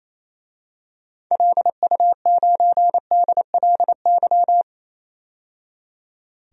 Morse code, heard as a keyed tone, about 28 wpm, 700 Hz. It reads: LU9DLY
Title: DXCC-100 Call Signs - 6 Characters - 28wpm